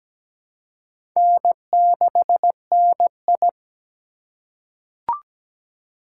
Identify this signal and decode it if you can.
Morse code, heard as a keyed tone, about 17 wpm, 700 Hz. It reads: N6NI
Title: USA Call Signs - 4x Character Spacing - 17wpm